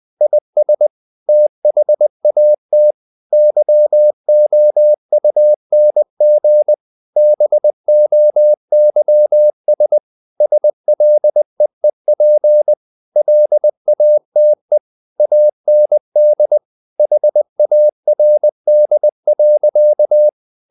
Morse code, heard as a keyed tone, about 20 wpm, 600 Hz. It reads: IS THAT YOUNG BOYS SLEEP LATE AND HARD.
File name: the old man and the sea / the old man and the sea part 1 20wpm